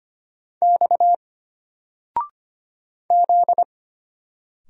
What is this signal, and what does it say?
Morse code, heard as a keyed tone, about 25 wpm, 700 Hz. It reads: X Z
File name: Single-Letter-Number-25wpm